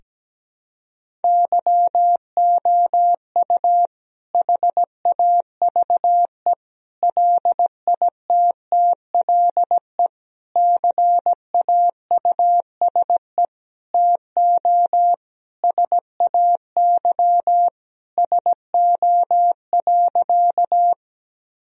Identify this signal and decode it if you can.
Morse code, heard as a keyed tone, about 17 wpm, 700 Hz. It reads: YOU HAVE LITTLE CAUSE TO SAY SO.
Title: Sentences from Top 1000 Words - Review Part 1 - 17wpm